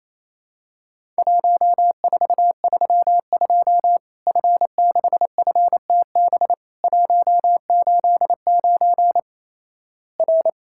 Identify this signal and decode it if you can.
Morse code, heard as a keyed tone, about 27 wpm, 700 Hz. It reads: 1432 F6FTB 189 R
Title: Worked All Europe DX Contest - QTC Exchange - 28wpm